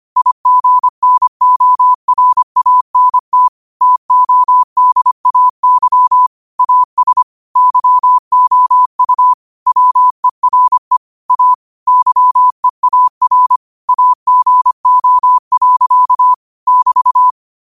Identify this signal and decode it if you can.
Morse code, heard as a keyed tone, about 25 wpm, 1000 Hz. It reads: IGNORANT TODAY AS YOU WERE A YEAR AGO. <BT>
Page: New quotes every day in morse code at 25 Words per minute.